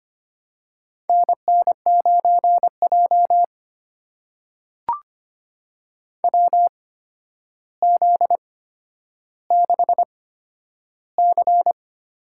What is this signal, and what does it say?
Morse code, heard as a keyed tone, about 25 wpm, spaced as for 11 wpm, 700 Hz. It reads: NN9J WZ6C
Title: USA Call Signs - 4 Characters - 8x Character Spacing - 25wpm